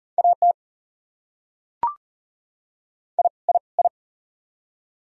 Morse code, heard as a keyed tone, about 40 wpm, 700 Hz. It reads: AT I I I
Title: Top 100 Words Repeated 40wpm